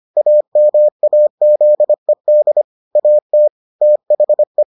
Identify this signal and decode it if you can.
Morse code, heard as a keyed tone, about 25 wpm, 600 Hz. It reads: AMAZED AT THE